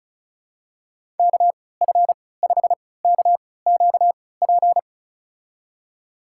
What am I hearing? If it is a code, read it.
Morse code, heard as a keyed tone, about 36 wpm, spaced as for 20 wpm, 700 Hz. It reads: KF5KQP
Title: USA Call Signs - 6 Characters - 3x Character Spacing - 35wpm